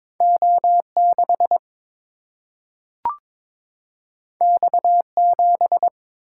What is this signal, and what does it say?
Morse code, heard as a keyed tone, about 22 wpm, 700 Hz. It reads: O 6 X 7
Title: Single-Letter Single-Number 22wpm